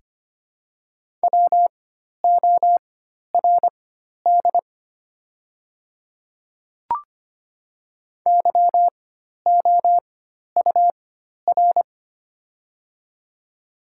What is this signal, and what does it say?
Morse code, heard as a keyed tone, about 25 wpm, spaced as for 12 wpm, 700 Hz. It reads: WORD YOUR
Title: 4x Character Spacing - Top 100 Words - 25wpm